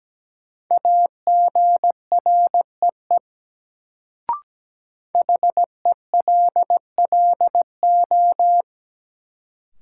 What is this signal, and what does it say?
Morse code, heard as a keyed tone, about 17 wpm, 700 Hz. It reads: AGREE HELLO
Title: Sentences from Top 1100 Words - New Word Drill Alt - 17wpm